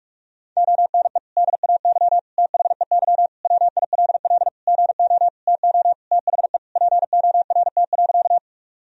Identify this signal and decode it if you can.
Morse code, heard as a keyed tone, about 43 wpm, 700 Hz. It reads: ONE DAY THEY WILL GO TO THE PORT.
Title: Sentences from Top 1000 Words - Review Part 1 - 45wpm